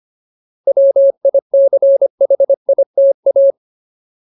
Morse code, heard as a keyed tone, about 25 wpm, 550 Hz. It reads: WICHITA
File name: Ultimate Word List / 25WPM Ultimate